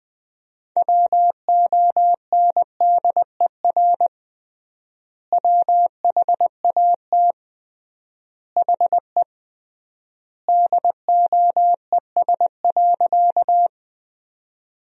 Morse code, heard as a keyed tone, about 20 wpm, 700 Hz. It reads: WONDER WHAT HE DOES.
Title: 3x Word-Spacing - Sentences from Top 500 Words - 20wpm